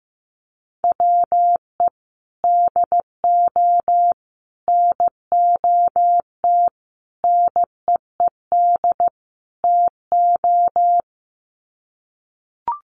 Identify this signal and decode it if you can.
Morse code, heard as a keyed tone, about 15 wpm, 700 Hz. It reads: WE DO NOT NEED TO
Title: Sets of 5 Words Encore 15wpm